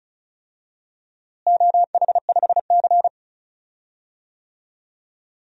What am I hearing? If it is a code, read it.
Morse code, heard as a keyed tone, about 36 wpm, 690 Hz. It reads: OH5C
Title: DXCC-100 Call Signs - 4 Characters - Repeated 3x - 35wpm